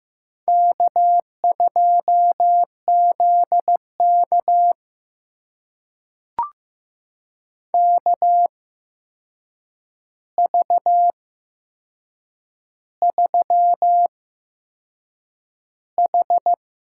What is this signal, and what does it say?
Morse code, heard as a keyed tone, about 15 wpm, spaced as for 6 wpm, 700 Hz. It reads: K2ZK KV3H
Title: USA Call Signs - 4 Characters - 8x Character Spacing - 15wpm